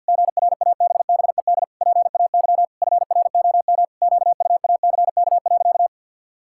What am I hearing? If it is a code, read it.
Morse code, heard as a keyed tone, about 50 wpm, 700 Hz. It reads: GRABBED PAY FROM QUACK.
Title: Pangram Challenge v2 - 50wpm